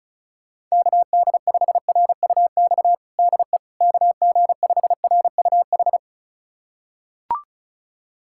Morse code, heard as a keyed tone, about 36 wpm, 700 Hz. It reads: KD5RUX DE KG5RUH
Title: USA Callsign DE Callsign - 6 Characters - 35wpm